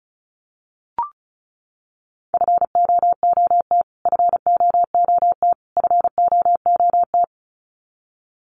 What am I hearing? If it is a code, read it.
Morse code, heard as a keyed tone, about 34 wpm, 700 Hz. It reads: FOOT FOOT FOOT
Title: Top 500 Words Repeated 35wpm